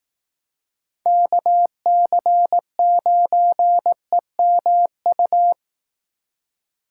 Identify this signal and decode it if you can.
Morse code, heard as a keyed tone, about 18 wpm, 700 Hz. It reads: KC9EMU
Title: Speed Racing - USA 2x3 Call Signs 15wpm